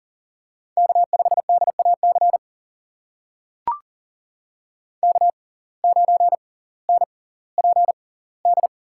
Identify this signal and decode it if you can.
Morse code, heard as a keyed tone, about 39 wpm, spaced as for 20 wpm, 700 Hz. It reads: K5DAC K9NPD
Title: USA Call Signs - 5 Characters - 6x Character Spacing - 40wpm